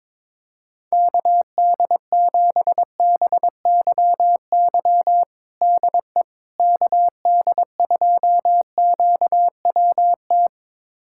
Morse code, heard as a keyed tone, about 22 wpm, 700 Hz. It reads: KD7BYY DE KD2QWT
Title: USA Callsign DE Callsign - 6 Characters - 22wpm